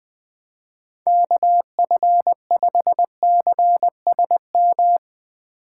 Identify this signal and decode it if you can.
Morse code, heard as a keyed tone, about 20 wpm, 700 Hz. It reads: KF5CSM
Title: Speed Racing - USA 2x3 Call Signs 20wpm